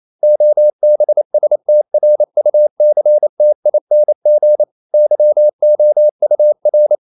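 Morse code, heard as a keyed tone, about 28 wpm, 600 Hz. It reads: OBSTRUCTING YOUR
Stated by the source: Morse code practice at 28 wpm, text of Ken Follett's novel Never in English, from CW med Gnister.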